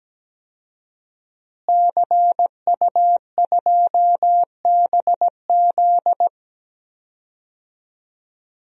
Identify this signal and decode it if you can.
Morse code, heard as a keyed tone, about 17 wpm, 700 Hz. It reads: CU2BZ
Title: DXCC-100 Call Signs - 5 Characters - Repeated 2x - 17wpm